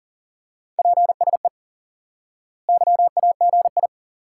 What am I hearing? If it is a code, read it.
Morse code, heard as a keyed tone, about 37 wpm, 700 Hz. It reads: PSE YAGI
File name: IQE Course 3 - Lesson 89 - YAGI - Two Elements - 40wpm